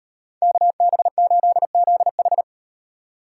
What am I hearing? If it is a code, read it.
Morse code, heard as a keyed tone, about 38 wpm, 700 Hz. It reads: KB8ZH